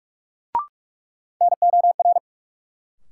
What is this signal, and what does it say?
Morse code, heard as a keyed tone, about 42 wpm, 700 Hz. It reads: NOR
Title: Sentences from Top 1500 Words - New Word Drill Alt - 45wpm